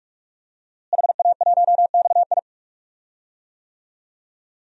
Morse code, heard as a keyed tone, about 44 wpm, 700 Hz. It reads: HA1XI
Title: DXCC-100 Call Signs - 5 Characters - Repeated 3x - 45wpm